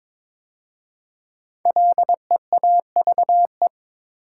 Morse code, heard as a keyed tone, about 22 wpm, 690 Hz. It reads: LEAVE
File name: 5 Letter Words - First 3 Letters Given - 22wpm